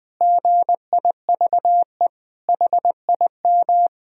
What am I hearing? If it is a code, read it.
Morse code, heard as a keyed tone, about 20 wpm, 700 Hz. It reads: GIVE HIM
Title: Sentences from Top 1000 Words - Review Part 4 - 20wpm